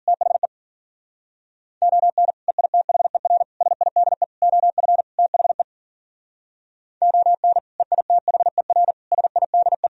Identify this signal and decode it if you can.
Morse code, heard as a keyed tone, about 52 wpm, 700 Hz. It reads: THE ON EITHER SIDE OF THE ON EITHER SIDE